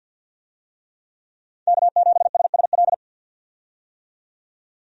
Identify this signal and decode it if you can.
Morse code, heard as a keyed tone, about 48 wpm, 700 Hz. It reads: K7SSL
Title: USA Call Signs - 5 Characters - 50wpm